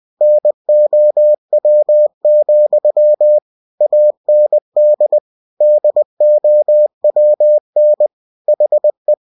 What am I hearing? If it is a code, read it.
Morse code, heard as a keyed tone, about 20 wpm, 600 Hz. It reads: NOW, AND DOWN HE
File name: cw-books / winnie-the-pooh / winnie-20wpm-page-0053